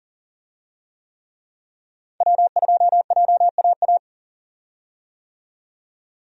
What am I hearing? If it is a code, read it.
Morse code, heard as a keyed tone, about 40 wpm, 700 Hz. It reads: W2JAA
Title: USA Call Signs - 5 Characters - 40wpm